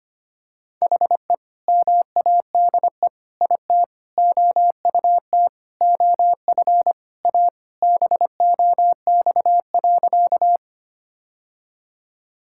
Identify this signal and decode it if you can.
Morse code, heard as a keyed tone, about 25 wpm, 700 Hz. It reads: HE MADE IT OUT OF A BOX.
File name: practice / 25wpm / Sentences with 3-Word Phrases - 25wpm